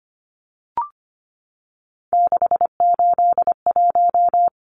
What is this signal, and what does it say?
Morse code, heard as a keyed tone, about 25 wpm, 700 Hz. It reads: 681
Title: Numbers - 3-Digit 25wpm